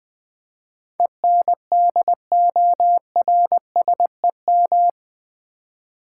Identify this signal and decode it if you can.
Morse code, heard as a keyed tone, about 20 wpm, 700 Hz. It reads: ENDORSEM
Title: Build A Word - 4 Letters At A Time - 11-Letter Words - 20wpm